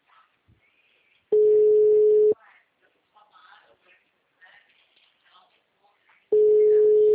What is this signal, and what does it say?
Ringback tone (Europe, 425 Hz), with speech.